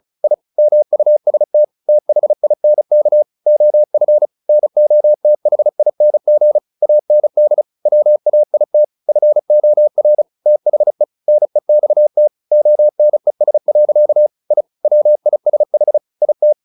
Morse code, heard as a keyed tone, about 34 wpm, 600 Hz. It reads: I MUST THINK OF NOTHING AND WAIT FOR THE NEXT ONES. I WISH IT
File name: the old man and the sea 2 35wpm